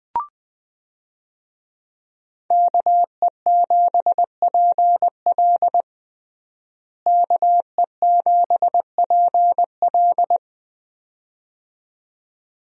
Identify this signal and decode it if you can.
Morse code, heard as a keyed tone, about 20 wpm, 700 Hz. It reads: KE7PL KE7PL
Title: USA Call Signs - 5 Characters - Repeated 2x - 20wpm